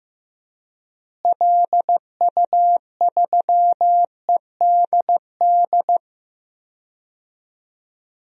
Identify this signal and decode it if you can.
Morse code, heard as a keyed tone, about 15 wpm, 700 Hz. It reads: LU3EDD
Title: DXCC-100 Call Signs - 6 Characters - Repeated 2x - 15wpm